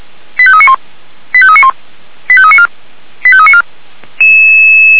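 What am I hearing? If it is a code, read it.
Five-tone selective call (ZVEI), heard as quick runs of five tones, tones of about 70 ms. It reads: 85391 85391 85394 85394
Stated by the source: ZVEI five-tone call received by SDR.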